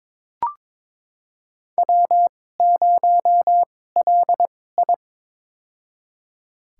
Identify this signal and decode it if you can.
Morse code, heard as a keyed tone, about 22 wpm, spaced as for 16 wpm, 700 Hz. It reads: W0LI
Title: USA Call Signs - 4 Characters - 2x Character Spacing - 22wpm